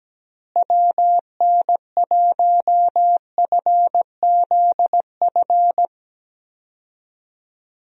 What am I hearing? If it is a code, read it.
Morse code, heard as a keyed tone, about 17 wpm, 700 Hz. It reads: WN1FZF
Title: USA Call Signs - 6 Characters - Repeated 2x - 17wpm